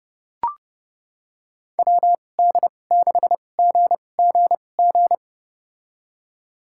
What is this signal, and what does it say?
Morse code, heard as a keyed tone, about 30 wpm, spaced as for 22 wpm, 700 Hz. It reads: WD6GGG
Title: USA Call Signs - 6 Characters - 2x Character Spacing - 30wpm